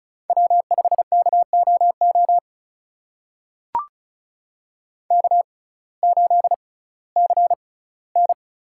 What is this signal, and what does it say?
Morse code, heard as a keyed tone, about 36 wpm, spaced as for 19 wpm, 700 Hz. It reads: W5KOO K8CN
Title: USA Call Signs - 6x Character Spacing - 35wpm